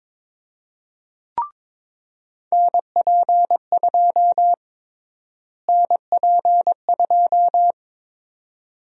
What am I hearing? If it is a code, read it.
Morse code, heard as a keyed tone, about 22 wpm, 700 Hz. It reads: NP2 NP2
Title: USA Call Sign Prefixes - Repeated 2x - 22wpm